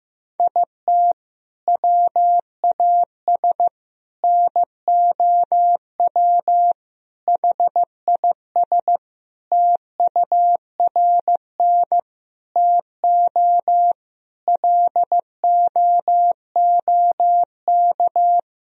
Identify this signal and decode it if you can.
Morse code, heard as a keyed tone, about 16 wpm, 700 Hz. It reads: IT WAS NOW HIS TURN TO LOOK
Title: Sentences from Top 200 Words and Abbreviations 15wpm